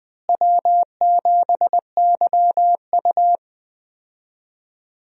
Morse code, heard as a keyed tone, about 20 wpm, 700 Hz. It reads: W7YU
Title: USA Call Signs - 4 Characters - Repeated 3x - 20wpm